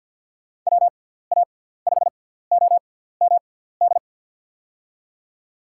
Morse code, heard as a keyed tone, about 51 wpm, spaced as for 17 wpm, 700 Hz. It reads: WA5OMD
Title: USA Call Signs - 6 Characters - 6x Character Spacing - 50wpm